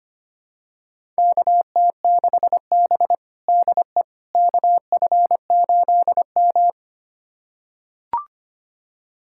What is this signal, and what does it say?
Morse code, heard as a keyed tone, about 25 wpm, 700 Hz. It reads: KT6B DE KF8M
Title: USA Callsign DE Callsign - 4 Characters - 25wpm